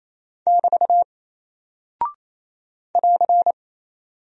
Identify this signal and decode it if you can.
Morse code, heard as a keyed tone, about 28 wpm, 700 Hz. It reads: <BT> <AR>
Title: Single Character (Letter:Number:Punctuation:Prosign) Alt 28wpm